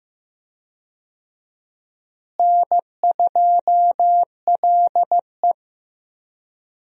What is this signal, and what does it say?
Morse code, heard as a keyed tone, about 15 wpm, 700 Hz. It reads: N2LE
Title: USA Call Signs - 4 Characters - Repeated 3x - 15wpm